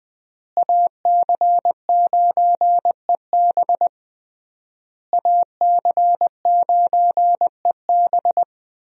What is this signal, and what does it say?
Morse code, heard as a keyed tone, about 20 wpm, 700 Hz. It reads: AC9EB AC9EB
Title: USA Call Signs - 5 Characters - Repeated 3x - 20wpm